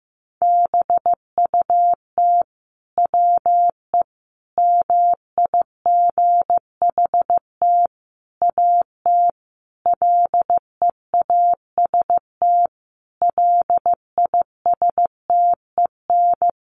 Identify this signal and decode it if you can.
Morse code, heard as a keyed tone, about 15 wpm, 700 Hz. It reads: BUT WE MIGHT AT LEAST LISTEN
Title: Sentences from Top 1300 Words 15wpm